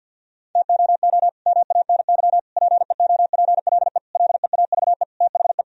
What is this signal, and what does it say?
Morse code, heard as a keyed tone, about 49 wpm, 700 Hz. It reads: TOO MANY PEOPLE LEAVE THE
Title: Sentences from Top 1000 Words - Review Part 1 - 50wpm